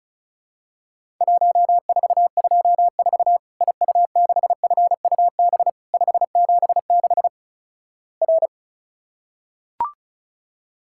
Morse code, heard as a keyed tone, about 36 wpm, 700 Hz. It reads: 1424 IU6FUB 576 R
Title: Worked All Europe DX Contest - QTC Exchange - 35wpm